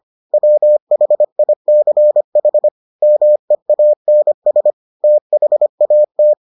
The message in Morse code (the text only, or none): WHICH MEANS THAT